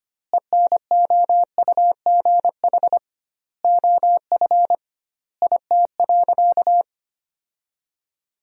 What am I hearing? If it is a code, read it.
Morse code, heard as a keyed tone, about 25 wpm, 700 Hz. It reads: ENOUGH OF IT.
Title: 2x Word-Spacing - Sentences from Top 400 Words - 25wpm